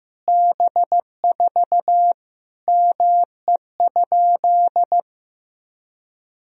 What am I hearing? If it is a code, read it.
Morse code, heard as a keyed tone, about 15 wpm, 700 Hz. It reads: B4 ME?
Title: Sentences from Top 200 Words and Abbreviations 15wpm